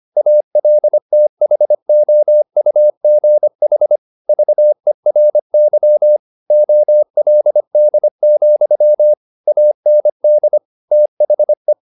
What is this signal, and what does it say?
Morse code, heard as a keyed tone, about 25 wpm, 600 Hz. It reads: ALTHOUGH VERY OLD, AND THE
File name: the old man and the sea part 1 25wpm